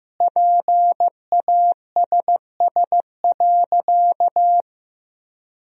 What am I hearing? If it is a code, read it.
Morse code, heard as a keyed tone, about 15 wpm, 700 Hz. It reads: PASS.
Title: 4x Word-Spacing - Sentences from Top 400 Words - 15wpm